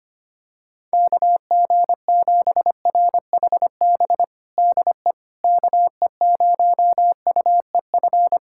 Morse code, heard as a keyed tone, about 25 wpm, 700 Hz. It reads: KG7RHB DE KE0UEF
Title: USA Callsign DE Callsign - 6 Characters - 25wpm